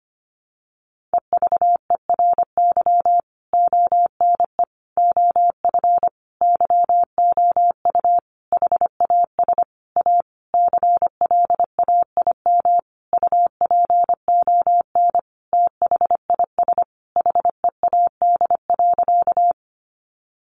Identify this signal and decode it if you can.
Morse code, heard as a keyed tone, about 25 wpm, 700 Hz. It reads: EVERY ONE OF YOU HAS A CLAIM UPON THIS HEAD.
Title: Sentences from Top 1500 Words 25wpm